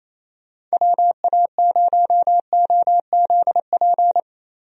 Morse code, heard as a keyed tone, about 28 wpm, 700 Hz. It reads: WA0OZP
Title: USA Call Signs - 6 Characters - Repeated 2x - 28wpm